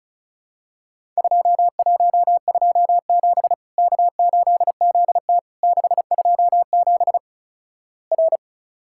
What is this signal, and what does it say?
Morse code, heard as a keyed tone, about 36 wpm, 700 Hz. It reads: 2127 K8ZT 627 R
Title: Worked All Europe DX Contest - QTC Exchange - 35wpm